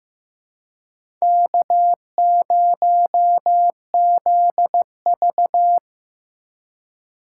Morse code, heard as a keyed tone, about 15 wpm, 700 Hz. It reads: K0ZV
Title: USA Call Signs - 4 Characters - 6x Character Spacing - 15wpm